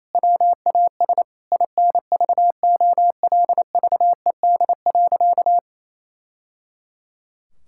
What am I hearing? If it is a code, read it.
Morse code, heard as a keyed tone, about 27 wpm, 700 Hz. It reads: WAS INVOLVED.
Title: Sentences from Top 1000 Words - Review Part 5 - 28wpm